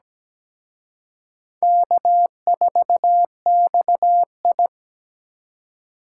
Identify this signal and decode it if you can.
Morse code, heard as a keyed tone, about 20 wpm, 700 Hz. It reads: K4XI
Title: Speed Racing - USA 1x2 Call Signs 17wpm